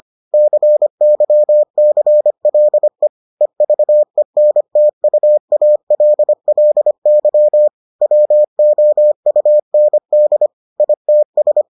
Morse code, heard as a keyed tone, about 25 wpm, 600 Hz. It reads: CYCLE EVENTUALLY WOUND ITS